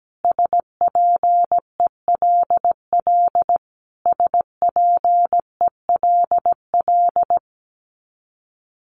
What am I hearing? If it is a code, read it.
Morse code, heard as a keyed tone, about 17 wpm, 700 Hz. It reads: SPELL SPELL
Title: Top 500 Words Repeated 17wpm